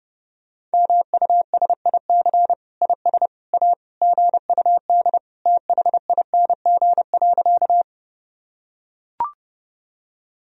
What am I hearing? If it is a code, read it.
Morse code, heard as a keyed tone, about 30 wpm, 700 Hz. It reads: MUSIC IS A GUD THING.
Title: Sentences from Top 300 Words and Abbreviations 30wpm